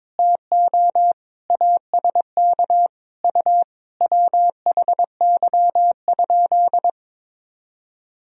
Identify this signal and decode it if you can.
Morse code, heard as a keyed tone, about 22 wpm, 700 Hz. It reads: TO ASK U WHY?
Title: Sentences from Top 200 Words and Abbreviations 22wpm